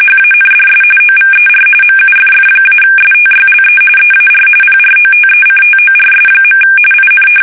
RTTY, about 45.45 baud. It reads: HANDS OVER MUFF-STYLE EAR PROTECTORS. IN
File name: fsk850 / battleshipnj